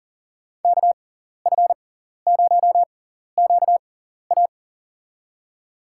Morse code, heard as a keyed tone, about 40 wpm, spaced as for 14 wpm, 700 Hz. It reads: KF0QA
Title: USA Call Signs - 6x Character Spacing - 40wpm